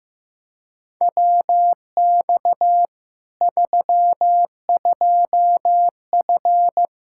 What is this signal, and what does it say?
Morse code, heard as a keyed tone, about 15 wpm, 700 Hz. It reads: WX 32F
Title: Practice QSO - Exchange 2 - Info - 15wpm